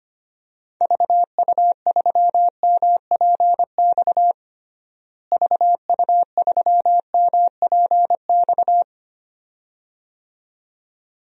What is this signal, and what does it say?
Morse code, heard as a keyed tone, about 25 wpm, 700 Hz. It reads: VU3MPX VU3MPX
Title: DXCC-100 Call Signs - 6 Characters - Repeated 2x - 25wpm